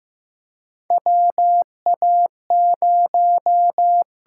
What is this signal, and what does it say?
Morse code, heard as a keyed tone, about 15 wpm, 700 Hz. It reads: WA0
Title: USA Call Signs - 6 Characters - 8x Character Spacing - 15wpm